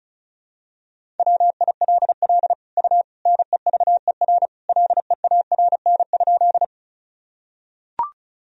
Morse code, heard as a keyed tone, about 36 wpm, 700 Hz. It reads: WILL U NEVER LEARN?
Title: Sentences from Top 300 Words and Abbreviations 35wpm